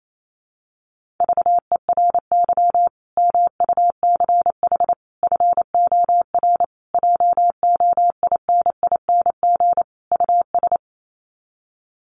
Morse code, heard as a keyed tone, about 28 wpm, 700 Hz. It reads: VERY MUCH FOR JOINING US
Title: Sets of 5 Words 28wpm